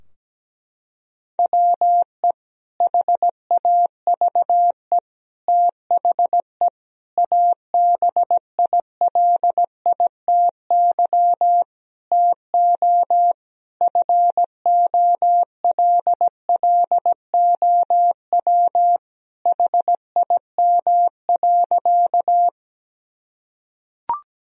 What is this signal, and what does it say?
Morse code, heard as a keyed tone, about 17 wpm, 700 Hz. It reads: WE HAVE THE ABILITY TO FOLLOW HIM.
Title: Sentences from Top 1000 Words - Review Part 3 - 17wpm